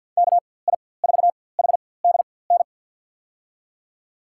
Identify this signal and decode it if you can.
Morse code, heard as a keyed tone, about 52 wpm, spaced as for 23 wpm, 700 Hz. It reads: KI4HDN
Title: USA Call Signs - 6 Characters - 4x Character Spacing - 50wpm